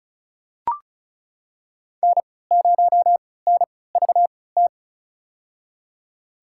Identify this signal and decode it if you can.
Morse code, heard as a keyed tone, about 34 wpm, spaced as for 20 wpm, 700 Hz. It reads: N0NVT
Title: USA Call Signs - 5 Characters - 3x Character Spacing - 35wpm